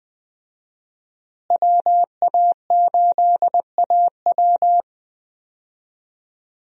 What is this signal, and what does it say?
Morse code, heard as a keyed tone, about 20 wpm, 700 Hz. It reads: WA8AW
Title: USA Call Signs - 5 Characters - Repeated 2x - 20wpm